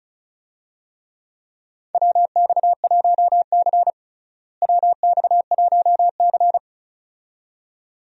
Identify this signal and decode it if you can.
Morse code, heard as a keyed tone, about 34 wpm, 700 Hz. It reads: WX1C WX1C
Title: USA Call Signs - 4 Characters - Repeated 2x - 35wpm